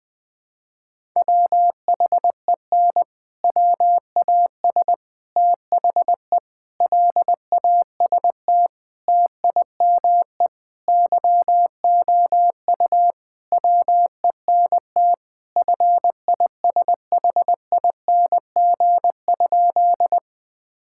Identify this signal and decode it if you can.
Morse code, heard as a keyed tone, about 21 wpm, 700 Hz. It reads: WHEN WAS THE LAST TIME YOU WENT FISHING?
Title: Sentences from Top 1000 Words - Review Part 5 - 20wpm